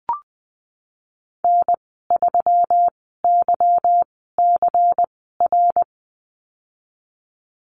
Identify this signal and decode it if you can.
Morse code, heard as a keyed tone, about 20 wpm, 700 Hz. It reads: N 3 Y C R
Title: 154 Lesson 25g - Letter p - Call Signs 20wpm